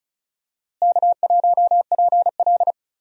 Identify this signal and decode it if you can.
Morse code, heard as a keyed tone, about 36 wpm, 700 Hz. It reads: K1PL
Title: USA Call Signs - 4 Characters - Repeated 2x - 35wpm